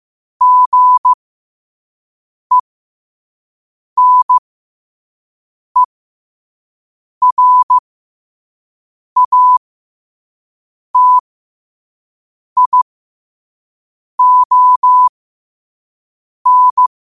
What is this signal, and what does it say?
Morse code, heard as a keyed tone, about 15 wpm, spaced as for 5 wpm, 1000 Hz. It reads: GENERATION
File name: qotdfiles / 5wpm